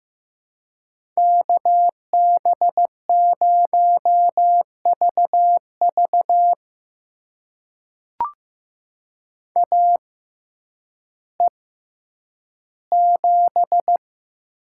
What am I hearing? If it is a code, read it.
Morse code, heard as a keyed tone, about 15 wpm, spaced as for 9 wpm, 700 Hz. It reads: KB0VV AE7
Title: USA Call Signs - 5 Characters - 6x Character Spacing - 15wpm